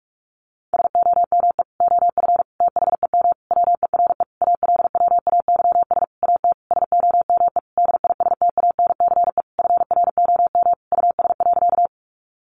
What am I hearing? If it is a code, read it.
Morse code, heard as a keyed tone, about 46 wpm, 700 Hz. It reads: SOME OF THEM WERE ALWAYS AT SOME DISTANCE FROM US.
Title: Sentences from Top 1200 Words 45wpm